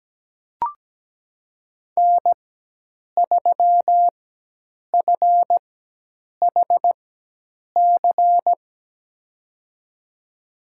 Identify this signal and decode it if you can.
Morse code, heard as a keyed tone, about 17 wpm, spaced as for 8 wpm, 700 Hz. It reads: N3FHC
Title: USA Call Signs - 5 Characters - 4x Character Spacing - 17wpm